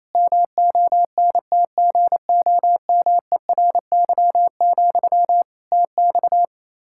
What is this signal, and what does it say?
Morse code, heard as a keyed tone, about 28 wpm, 700 Hz. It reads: MONTGOMERY, TX
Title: Practice QSO - Exchange 1 - Beginning - 28wpm